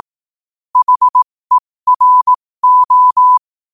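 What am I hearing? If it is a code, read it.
Morse code, heard as a keyed tone, about 18 wpm, spaced as for 15 wpm, 1000 Hz. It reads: HERO
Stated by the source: Morse quote-of-the-day practice at 15 WPM from Morse Resource.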